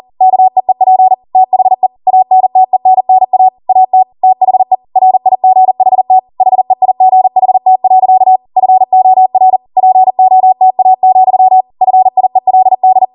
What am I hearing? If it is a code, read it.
Morse code, heard as a keyed tone, about 40 wpm, 750 Hz. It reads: KEEP THE ANTENNA AT THE RIGHT HEIGHT. FOR POTA, FIELD